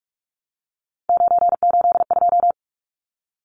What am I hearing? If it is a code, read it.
Morse code, heard as a keyed tone, about 44 wpm, 700 Hz. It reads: 982
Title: Numbers - 3-Digit 45wpm